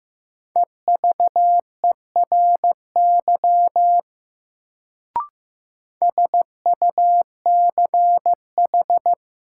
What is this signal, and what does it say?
Morse code, heard as a keyed tone, about 15 wpm, 700 Hz. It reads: EVERY SUCH
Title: Sentences from Top 200 Words - New Word Drill - 15wpm